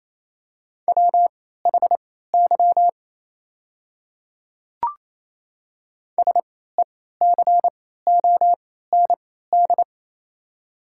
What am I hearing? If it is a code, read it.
Morse code, heard as a keyed tone, about 28 wpm, spaced as for 16 wpm, 700 Hz. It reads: WHY SECOND
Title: 3x Character Spacing - Top 500 Words - 28wpm